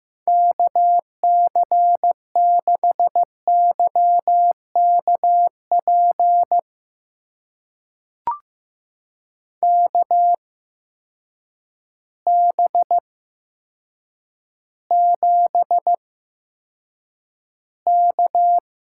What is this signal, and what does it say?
Morse code, heard as a keyed tone, about 15 wpm, spaced as for 7 wpm, 700 Hz. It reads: KC6YKP KB7K
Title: USA Call Signs - 6 Characters - 8x Character Spacing - 15wpm